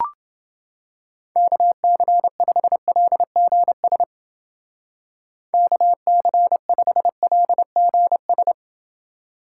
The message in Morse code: KC5LGS KC5LGS

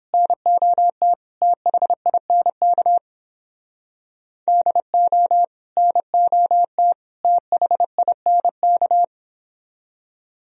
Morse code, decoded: NOT THINK DO NOT THINK

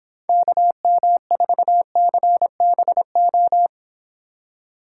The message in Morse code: KM4CBO

